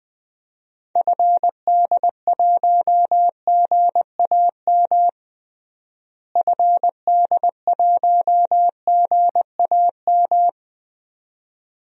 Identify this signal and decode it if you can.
Morse code, heard as a keyed tone, about 20 wpm, 690 Hz. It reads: FD1GAM FD1GAM
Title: DXCC-100 Call Signs - 6 Characters - Repeated 2x - 20wpm